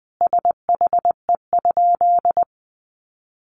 Morse code, heard as a keyed tone, about 20 wpm, 700 Hz. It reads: SHE?
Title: Sentences from Top 1200 Words 20wpm